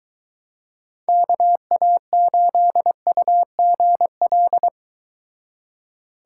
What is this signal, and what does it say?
Morse code, heard as a keyed tone, about 24 wpm, 700 Hz. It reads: KA8UGL